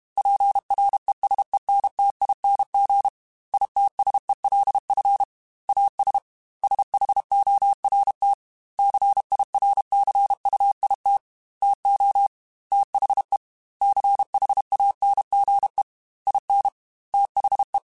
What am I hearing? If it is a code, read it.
Morse code, heard as a keyed tone, about 32 wpm, 800 Hz. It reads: PRESENTING ITSELF AS SHORT CIRCUIT TO THE CHANGE IN THE
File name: Technical Subjects / Technical Subjects-mp3s / XC @30